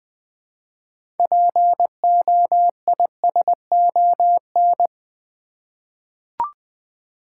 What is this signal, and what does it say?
Morse code, heard as a keyed tone, about 20 wpm, 700 Hz. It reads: POISON
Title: 6 Letter Words - First 3 Letters Given - 20wpm